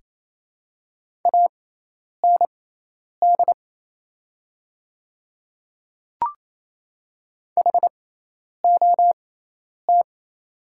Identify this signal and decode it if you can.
Morse code, heard as a keyed tone, about 28 wpm, spaced as for 10 wpm, 690 Hz. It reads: AND HOT
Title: 6x Character Spacing - Top 100 Words - 28wpm